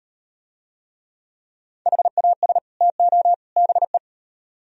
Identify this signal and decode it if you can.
Morse code, heard as a keyed tone, about 38 wpm, 700 Hz. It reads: HAS TO BE